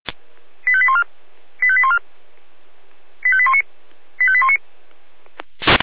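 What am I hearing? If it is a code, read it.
Five-tone selective call (ZVEI), heard as quick runs of five tones, tones of about 70 ms. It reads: 85814 85814 85819 85819
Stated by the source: ZVEI five-tone call received by SDR.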